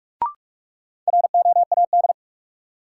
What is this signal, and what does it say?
Morse code, heard as a keyed tone, about 46 wpm, 700 Hz. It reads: ROAD
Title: Sets of 5 Words - New Word Drill - 45wpm